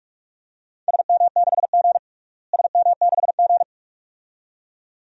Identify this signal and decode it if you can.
Morse code, heard as a keyed tone, about 46 wpm, 700 Hz. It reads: SM6G SM6G
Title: DXCC-100 Call Signs - 4 Characters - Repeated 2x - 45wpm